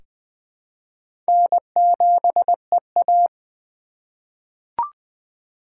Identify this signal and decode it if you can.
Morse code, heard as a keyed tone, about 20 wpm, 700 Hz. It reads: N7EA
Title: USA Call Signs - 4 Characters - 6x Character Spacing - 20wpm